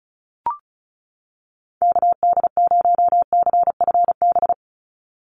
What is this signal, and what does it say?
Morse code, heard as a keyed tone, about 34 wpm, 700 Hz. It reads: KD0CFB
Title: USA Call Signs 35wpm